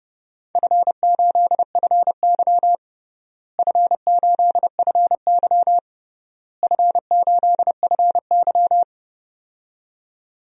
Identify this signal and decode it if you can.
Morse code, heard as a keyed tone, about 30 wpm, 700 Hz. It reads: F8FY F8FY F8FY
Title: DXCC-100 Call Signs - 4 Characters - Repeated 3x - 30wpm